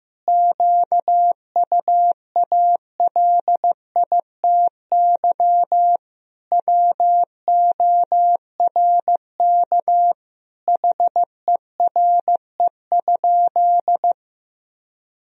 Morse code, heard as a keyed tone, about 15 wpm, 700 Hz. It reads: QUALITY WORK HERE?
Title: Sentences from Top 1000 Words - Review Part 4 - 15wpm